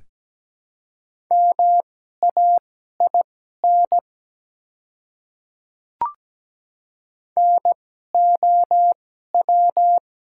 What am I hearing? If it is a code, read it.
Morse code, heard as a keyed tone, about 17 wpm, spaced as for 12 wpm, 700 Hz. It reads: MAIN NOW
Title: 2x Character Spacing - Top 300 Words - 17wpm